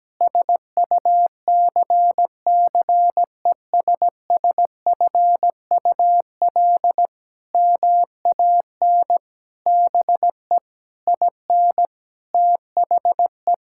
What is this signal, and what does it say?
Morse code, heard as a keyed tone, about 17 wpm, 700 Hz. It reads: SUCCESSFUL MAN BE IN THE
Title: Sentences from Top 1000 Words - Review Part 6 - 17wpm